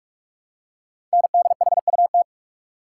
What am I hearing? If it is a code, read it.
Morse code, heard as a keyed tone, about 44 wpm, 700 Hz. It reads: NDHUT
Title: Code Groups - 5 Letters - 45wpm